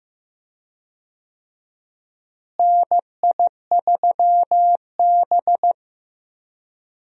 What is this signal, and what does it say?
Morse code, heard as a keyed tone, about 15 wpm, 700 Hz. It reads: NI3B
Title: USA Call Signs - 4 Characters - Repeated 2x - 15wpm